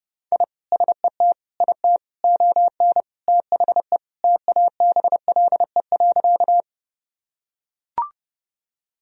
Morse code, heard as a keyed tone, about 30 wpm, 700 Hz. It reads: I SET IT ON THE TABLE.
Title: Sentences from Top 1000 Words - Review Part 2 - 30wpm